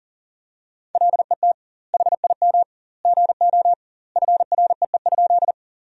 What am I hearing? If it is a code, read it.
Morse code, heard as a keyed tone, about 40 wpm, 700 Hz. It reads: LET HIM GO FREE?
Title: 2x Word-Spacing - Sentences from Top 500 Words - 40wpm